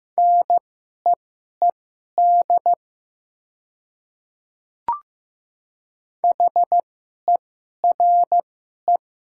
Morse code, heard as a keyed tone, about 15 wpm, spaced as for 11 wpm, 700 Hz. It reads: NEED HERE
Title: 2x Character Spacing - Top 500 Words - 15wpm